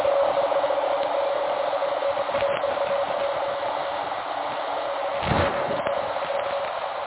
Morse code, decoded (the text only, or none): none